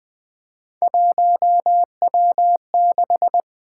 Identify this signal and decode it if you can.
Morse code, heard as a keyed tone, about 20 wpm, 700 Hz. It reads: 1W6
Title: Three Letter-Number Combinations - 20wpm